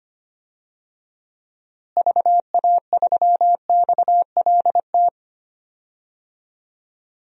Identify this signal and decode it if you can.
Morse code, heard as a keyed tone, about 25 wpm, 700 Hz. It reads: VA3XLT
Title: DXCC-100 Call Signs - 6 Characters - Repeated 3x - 25wpm